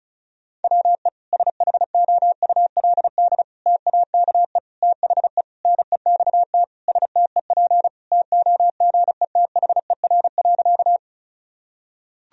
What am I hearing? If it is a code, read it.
Morse code, heard as a keyed tone, about 36 wpm, 700 Hz. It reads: WE SHOULD TAKE THE NEXT STEP TOGETHER.
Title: Sentences from Top 1000 Words - Review Part 2 - 35wpm